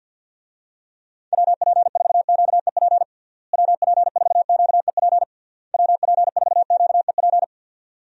Morse code, heard as a keyed tone, about 46 wpm, 700 Hz. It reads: WP4QEP WP4QEP WP4QEP
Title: USA Call Signs - 6 Characters - Repeated 3x - 50wpm